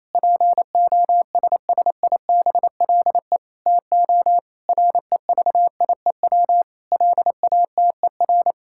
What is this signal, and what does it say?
Morse code, heard as a keyed tone, about 28 wpm, 700 Hz. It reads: POSSIBLE TO REVIEW LATER
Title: Sentences from Top 1000 Words - Review Part 6 - 28wpm